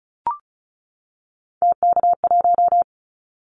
Morse code, heard as a keyed tone, about 34 wpm, 700 Hz. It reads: TK1
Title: Callsign Prefixes for Top 100 DXCC - 35wpm